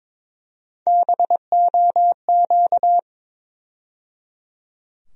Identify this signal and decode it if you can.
Morse code, heard as a keyed tone, about 22 wpm, 700 Hz. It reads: BOQ
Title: USA Call Sign Postfixes - 3 Letters 22wpm